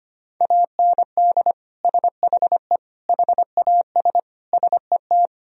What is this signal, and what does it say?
Morse code, heard as a keyed tone, about 25 wpm, 700 Hz. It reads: AND SHE HAS SET
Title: Sentences from Top 1000 Words - Review Part 4 - 25wpm